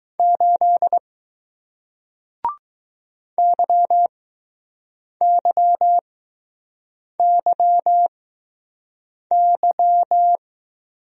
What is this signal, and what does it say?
Morse code, heard as a keyed tone, about 21 wpm, spaced as for 6 wpm, 700 Hz. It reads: 8 YYYY